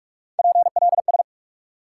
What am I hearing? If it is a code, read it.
Morse code, heard as a keyed tone, about 43 wpm, 700 Hz. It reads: PLS
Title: Top 180 QSO Elements 45wpm